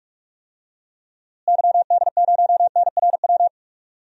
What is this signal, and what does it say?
Morse code, heard as a keyed tone, about 44 wpm, 700 Hz. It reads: YD0NRW
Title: DXCC-100 Call Signs - 6 Characters - Repeated 3x - 45wpm